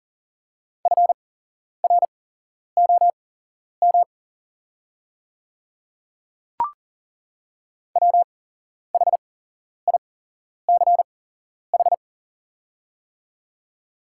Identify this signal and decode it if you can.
Morse code, heard as a keyed tone, about 38 wpm, spaced as for 11 wpm, 700 Hz. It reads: FROM WHICH
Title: 8x Character Spacing - Top 100 Words - 40wpm